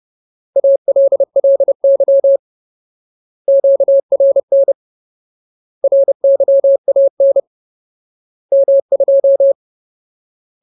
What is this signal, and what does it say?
Morse code, heard as a keyed tone, about 30 wpm, 550 Hz. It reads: ALLY QRN RYAN M2
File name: Ultimate Word List / 30WPM Ultimate